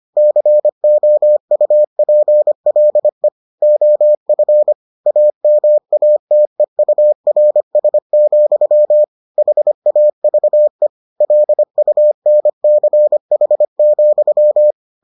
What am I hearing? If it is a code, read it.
Morse code, heard as a keyed tone, about 25 wpm, 600 Hz. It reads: COUPLE OF AMATEURS, HAVE LUNCH,